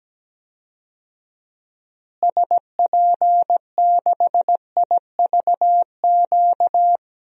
Morse code, heard as a keyed tone, about 17 wpm, 700 Hz. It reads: SP6IVQ
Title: DXCC-100 Call Signs - 6 Characters - Repeated 2x - 17wpm